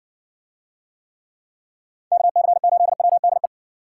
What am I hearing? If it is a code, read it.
Morse code, heard as a keyed tone, about 55 wpm, 700 Hz. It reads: KC8WDE